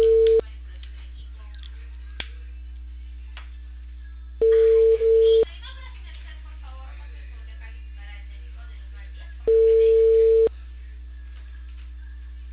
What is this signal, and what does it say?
Ringback tone (China, 450 Hz), with speech.